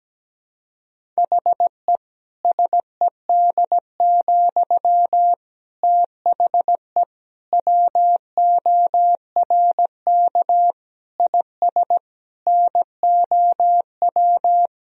Morse code, heard as a keyed tone, about 17 wpm, 700 Hz. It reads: HE SED, THE WORK IS NOW
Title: Sentences from Top 400 Words and Abbreviations 17wpm